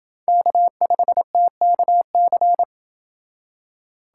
Morse code, heard as a keyed tone, about 29 wpm, 700 Hz. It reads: K5TKC